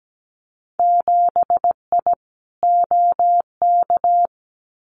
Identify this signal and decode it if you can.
Morse code, heard as a keyed tone, about 17 wpm, 700 Hz. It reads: 7I OK
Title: Winter Field Day Exchanges Encore 17wpm